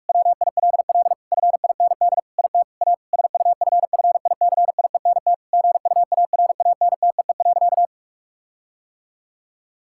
Morse code, heard as a keyed tone, about 46 wpm, 700 Hz. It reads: WILL FIND IT A SUFFICIENT GUARANTEE.
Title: Sentences from Top 1600 Words 45wpm